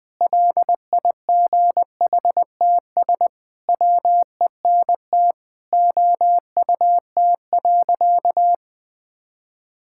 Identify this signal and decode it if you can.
Morse code, heard as a keyed tone, about 20 wpm, 700 Hz. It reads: LIGHTS WENT OUT.
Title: Sentences with 6-Word Phrases - 20wpm